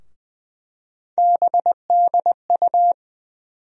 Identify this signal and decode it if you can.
Morse code, heard as a keyed tone, about 20 wpm, 700 Hz. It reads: BDU
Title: USA Call Sign Postfixes - 3 Letters 20wpm